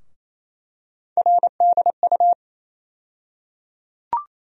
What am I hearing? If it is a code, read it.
Morse code, heard as a keyed tone, about 28 wpm, 700 Hz. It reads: RDU
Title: USA Call Sign Postfixes - 3 Letters 28wpm